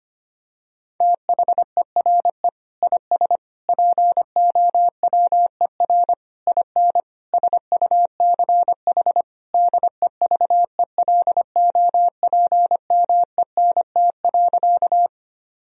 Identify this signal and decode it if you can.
Morse code, heard as a keyed tone, about 25 wpm, 700 Hz. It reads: THERE IS POWER IN SUCH DEVELOPMENT.
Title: Sentences from Top 1000 Words - Review Part 4 - 25wpm